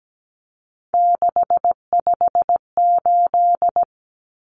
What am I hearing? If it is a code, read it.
Morse code, heard as a keyed tone, about 17 wpm, 700 Hz. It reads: 658
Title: Numbers - 3-Digit 17wpm